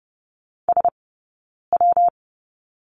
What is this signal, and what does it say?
Morse code, heard as a keyed tone, about 30 wpm, 700 Hz. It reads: S W
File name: Lesson 19d - Letter w - Two Characters 30wpm